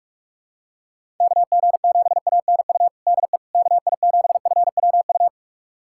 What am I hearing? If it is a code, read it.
Morse code, heard as a keyed tone, about 46 wpm, 700 Hz. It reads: KG7ANU DE KI7FWU
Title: USA Callsign DE Callsign - 6 Characters - 45wpm